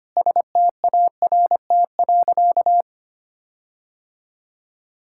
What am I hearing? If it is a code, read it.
Morse code, heard as a keyed tone, about 25 wpm, 700 Hz. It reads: START.
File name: practice / 25wpm / Sentences with 9-Word Phrases - 25wpm